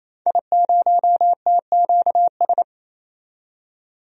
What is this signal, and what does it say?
Morse code, heard as a keyed tone, about 28 wpm, 700 Hz. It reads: I0TQS
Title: DXCC-100 Call Signs - 5 Characters - 28wpm